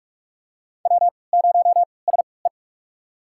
Morse code, heard as a keyed tone, about 46 wpm, spaced as for 25 wpm, 700 Hz. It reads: W0SE
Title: USA Call Signs - 3x Character Spacing - 45wpm